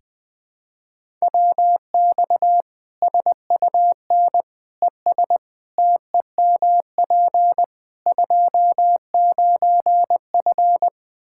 Morse code, heard as a keyed tone, about 20 wpm, 700 Hz. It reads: WX SUN ES TEMP 29F
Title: Practice QSO - Exchange 2 - Info - 20wpm